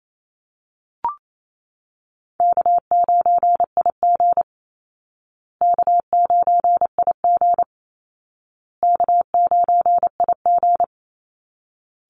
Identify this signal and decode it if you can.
Morse code, heard as a keyed tone, about 28 wpm, 700 Hz. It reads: K9IG K9IG K9IG
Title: Easy Call Signs Repeated 3x - 28wpm